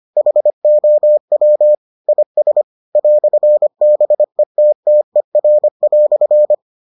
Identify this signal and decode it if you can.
Morse code, heard as a keyed tone, about 25 wpm, 600 Hz. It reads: HOW IS "BETTER"